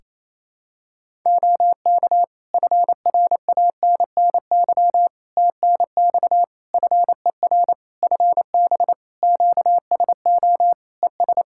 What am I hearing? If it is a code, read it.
Morse code, heard as a keyed tone, about 28 wpm, 700 Hz. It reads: OK FRANNY TNX FER FB QSO ES
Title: Practice QSO - Exchange 4 - Ending - 28wpm